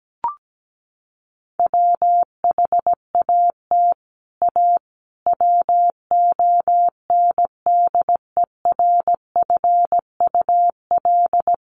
Morse code, heard as a keyed tone, about 17 wpm, 700 Hz. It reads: WHAT A WONDERFUL
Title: Sentences from Top 1100 Words 17wpm